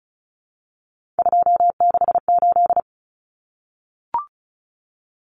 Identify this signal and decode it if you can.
Morse code, heard as a keyed tone, about 34 wpm, 700 Hz. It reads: 268
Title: Common Cut Numbers - 3-Digit 35wpm